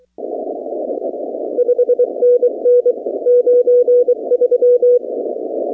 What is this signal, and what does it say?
Morse code, heard as a keyed tone, about 25 wpm, 500 Hz. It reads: 5NN 93
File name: 2019-cw-wpx